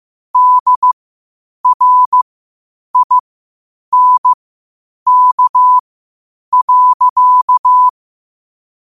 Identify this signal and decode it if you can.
Morse code, heard as a keyed tone, about 15 wpm, spaced as for 8 wpm, 1000 Hz. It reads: DRINK.
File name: qotdfiles / 8wpm